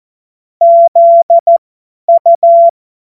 Morse code, heard as a keyed tone, about 14 wpm, spaced as for 10 wpm, 680 Hz. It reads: ZU
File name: Morsekurs / Witze / Witz 02. November 2021, 50 F70